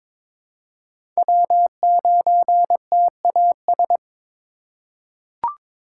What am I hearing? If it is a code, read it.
Morse code, heard as a keyed tone, about 22 wpm, 700 Hz. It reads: W9TAS
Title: USA Call Signs - 5 Characters - 2x Character Spacing - 22wpm